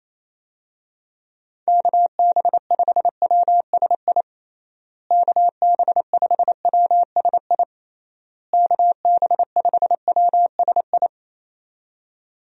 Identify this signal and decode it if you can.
Morse code, heard as a keyed tone, about 28 wpm, 700 Hz. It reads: KB5WSI KB5WSI KB5WSI
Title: USA Call Signs - 6 Characters - Repeated 3x - 28wpm